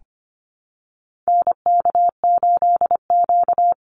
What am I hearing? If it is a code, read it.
Morse code, heard as a keyed tone, about 25 wpm, 700 Hz. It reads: NK8Q
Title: NA SOTA Activator Callsigns 25wpm